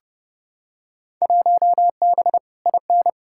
Morse code, heard as a keyed tone, about 29 wpm, 700 Hz. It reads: 1B IN
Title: Field Day Exchanges - 30wpm